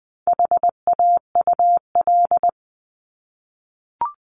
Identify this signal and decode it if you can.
Morse code, heard as a keyed tone, about 20 wpm, 700 Hz. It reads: HAUL
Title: 260 Lesson 42e - Words 20-20wpm